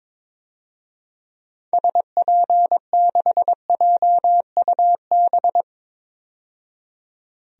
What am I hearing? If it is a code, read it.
Morse code, heard as a keyed tone, about 21 wpm, 700 Hz. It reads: SP6JUB
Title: DXCC-100 Call Signs - 6 Characters - 22wpm